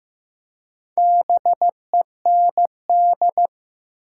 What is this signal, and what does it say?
Morse code, heard as a keyed tone, about 15 wpm, 700 Hz. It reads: BEND
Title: Four Letter Words - 15wpm